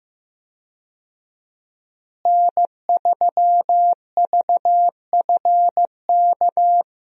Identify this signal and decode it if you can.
Morse code, heard as a keyed tone, about 15 wpm, 700 Hz. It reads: N3VFK
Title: USA Call Signs - 5 Characters - Repeated 2x - 15wpm